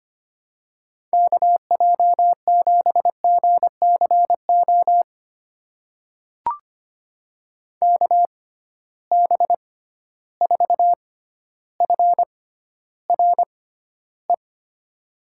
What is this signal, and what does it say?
Morse code, heard as a keyed tone, about 25 wpm, spaced as for 13 wpm, 700 Hz. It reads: KJ7GCO KB4FRE
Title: USA Call Signs - 6x Character Spacing - 25wpm